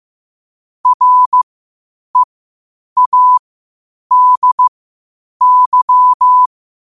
Morse code, heard as a keyed tone, about 15 wpm, spaced as for 8 wpm, 1000 Hz. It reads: READY